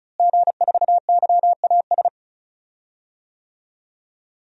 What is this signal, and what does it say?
Morse code, heard as a keyed tone, about 36 wpm, 700 Hz. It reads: G4YAS
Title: DXCC-100 Call Signs - 5 Characters - Repeated 2x - 35wpm